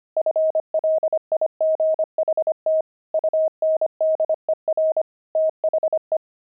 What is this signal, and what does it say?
Morse code, heard as a keyed tone, about 25 wpm, 630 Hz. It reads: FLIGHT UNDER THE
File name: pangram-fs-25-cs-25-beehive